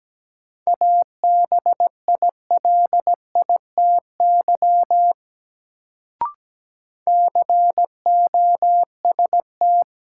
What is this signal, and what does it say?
Morse code, heard as a keyed tone, about 17 wpm, 700 Hz. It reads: ABILITY COST
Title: Sets of 3 Words - New Word Drill - 17wpm